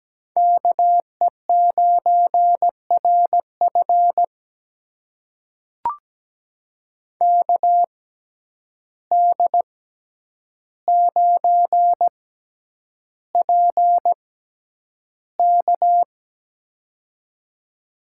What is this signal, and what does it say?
Morse code, heard as a keyed tone, about 17 wpm, spaced as for 9 wpm, 700 Hz. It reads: KE9RF KD9PK
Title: USA Call Signs - 5 Characters - 6x Character Spacing - 17wpm